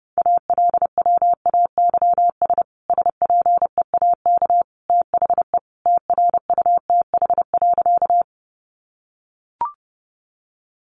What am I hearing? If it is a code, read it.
Morse code, heard as a keyed tone, about 30 wpm, 700 Hz. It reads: ALWAYS SPEAK THE TRUTH.
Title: Sentences from Top 1100 Words 30wpm